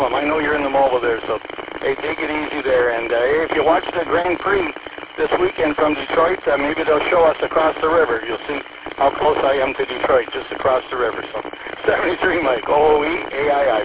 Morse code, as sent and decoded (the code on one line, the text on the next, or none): none